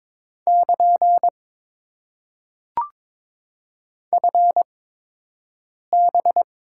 Morse code, heard as a keyed tone, about 22 wpm, spaced as for 6 wpm, 700 Hz. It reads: <KN> FB
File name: Top 90 QSO Elements - 8x Character Spacing - 22wpm